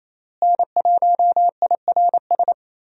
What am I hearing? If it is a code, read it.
Morse code, heard as a keyed tone, about 28 wpm, 700 Hz. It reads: N1IRS
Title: USA Call Signs - 5 Characters - Repeated 3x - 28wpm